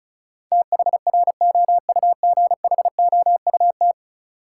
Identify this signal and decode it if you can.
Morse code, heard as a keyed tone, about 34 wpm, 700 Hz. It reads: THROUGHOUT
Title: Sentences from Top 800 Words - New Word Drill - 35wpm